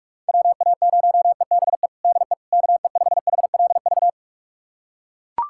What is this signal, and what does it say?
Morse code, heard as a keyed tone, about 44 wpm, 700 Hz. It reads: WA0EBE DE KE5HLV
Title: USA Callsign DE Callsign - 6 Characters - 45wpm